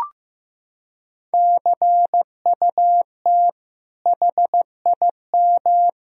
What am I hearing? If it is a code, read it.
Morse code, heard as a keyed tone, about 16 wpm, 700 Hz. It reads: CUT HIM
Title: Sentences from Top 1000 Words - Review Part 1 - 15wpm